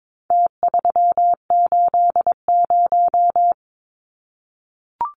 Morse code, heard as a keyed tone, about 22 wpm, 700 Hz. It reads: T380
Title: Numbers - Contest Style - 4-Digit 22wpm